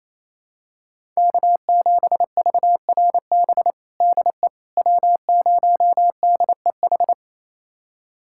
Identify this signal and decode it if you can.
Morse code, heard as a keyed tone, about 27 wpm, 700 Hz. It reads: K7VRB DE W0DEH
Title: USA Callsign DE Callsign - 5 Characters - 28wpm